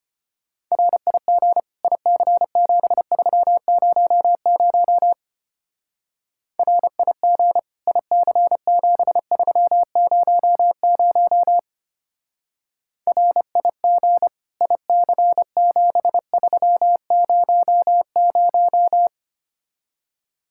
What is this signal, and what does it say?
Morse code, heard as a keyed tone, about 29 wpm, 700 Hz. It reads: RIG IC7300 RIG IC7300 RIG IC7300
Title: Speed-Racing Common QSO Phrases Encore - 25wpm